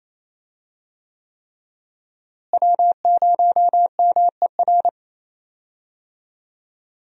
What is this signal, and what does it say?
Morse code, heard as a keyed tone, about 28 wpm, 700 Hz. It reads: W0MER
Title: USA Call Signs - 5 Characters - Repeated 3x - 28wpm